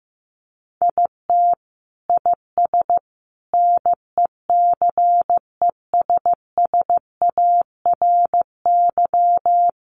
Morse code, heard as a keyed tone, about 15 wpm, 700 Hz. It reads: IT IS NECESSARY
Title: Sentences from Top 800 Words 15wpm